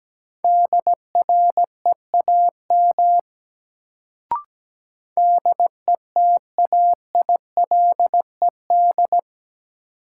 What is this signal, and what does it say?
Morse code, heard as a keyed tone, about 17 wpm, 700 Hz. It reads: DREAM DETAILED
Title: Sentences from Top 1500 Words - New Word Drill Alt - 17wpm